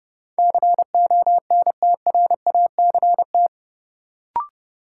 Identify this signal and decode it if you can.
Morse code, heard as a keyed tone, about 30 wpm, 700 Hz. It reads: CONTRACT
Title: Sentences from Top 1400 Words - New Word Drill - 30wpm